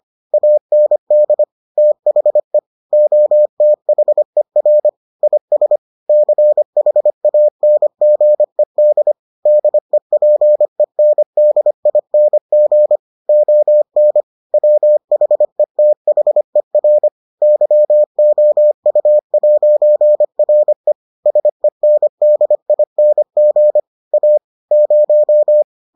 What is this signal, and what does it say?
Morse code, heard as a keyed tone, about 25 wpm, 600 Hz. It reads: AND THE OTHER IS CHANGED DEPENDING ON WHETHER YOU'RE SENDING A 0